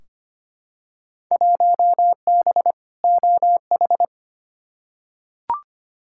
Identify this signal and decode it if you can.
Morse code, heard as a keyed tone, about 25 wpm, 700 Hz. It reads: 1B OH
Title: Field Day Exchanges - 25wpm